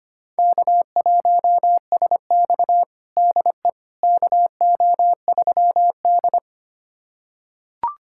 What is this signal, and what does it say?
Morse code, heard as a keyed tone, about 25 wpm, 700 Hz. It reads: K1SX DE KO3D
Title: USA Callsign DE Callsign - 4 Characters - 25wpm